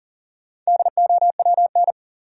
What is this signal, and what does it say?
Morse code, heard as a keyed tone, about 40 wpm, 700 Hz. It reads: DOWN
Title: Sentences from Top 300 Words - New Word Drill - 40wpm